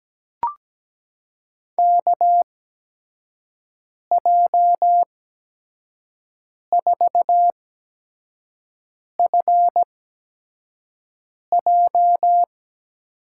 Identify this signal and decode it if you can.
Morse code, heard as a keyed tone, about 17 wpm, spaced as for 5 wpm, 700 Hz. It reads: KJ4FJ
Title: USA Call Signs - 8x Character Spacing - 17wpm